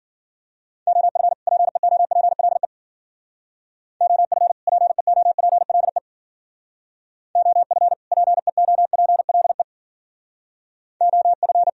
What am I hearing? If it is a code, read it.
Morse code, heard as a keyed tone, about 55 wpm, 700 Hz. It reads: OF PEOPLE OF PEOPLE OF PEOPLE OF